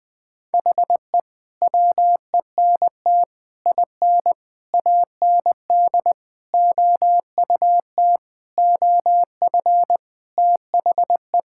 Morse code, heard as a keyed tone, about 20 wpm, 700 Hz. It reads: HE WENT IN AND OUT OF THE
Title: Sentences with 5-Word Phrases - 20wpm